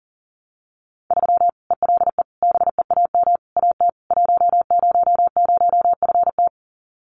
Morse code, heard as a keyed tone, about 40 wpm, 700 Hz. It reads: 3 ELE BEAM AT 100FT
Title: Common QSO Phrases 40wpm